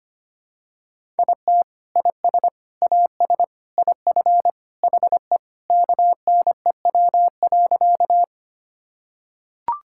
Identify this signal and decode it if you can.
Morse code, heard as a keyed tone, about 25 wpm, 700 Hz. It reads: IT IS AS IF HE KNEW.
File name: practice / 25wpm / Sentences with 3-Word Phrases - 25wpm